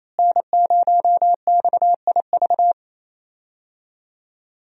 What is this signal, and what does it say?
Morse code, heard as a keyed tone, about 28 wpm, 700 Hz. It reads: N0XIV
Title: USA Call Signs - 5 Characters - Repeated 3x - 28wpm